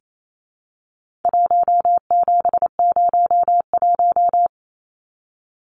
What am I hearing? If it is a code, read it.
Morse code, heard as a keyed tone, about 28 wpm, 700 Hz. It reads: 1701
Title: Numbers - 4-Digit 28wpm